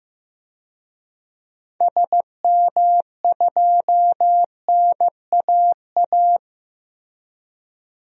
Morse code, heard as a keyed tone, about 15 wpm, 700 Hz. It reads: SM2NAA
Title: DXCC-100 Call Signs - 6 Characters - Repeated 3x - 15wpm